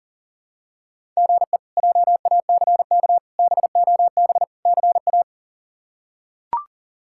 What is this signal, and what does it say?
Morse code, heard as a keyed tone, about 40 wpm, 700 Hz. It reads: GE JACK BOB CA
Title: K1USN SST QSO Practice - 40wpm